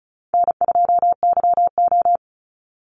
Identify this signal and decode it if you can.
Morse code, heard as a keyed tone, about 34 wpm, 700 Hz. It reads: N2YO
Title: Easy Call Signs Repeated 3x - 35wpm